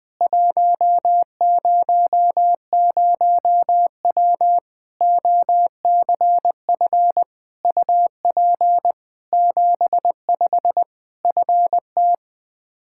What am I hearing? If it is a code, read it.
Morse code, heard as a keyed tone, about 20 wpm, 700 Hz. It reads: 100W OCF UP 75 FT
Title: K1USN SST QSO Practice 20wpm